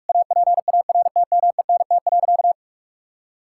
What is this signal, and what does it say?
Morse code, heard as a keyed tone, about 41 wpm, 700 Hz. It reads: APARTMENT.
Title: Sentences from Top 1000 Words - Review Part 6 - 45wpm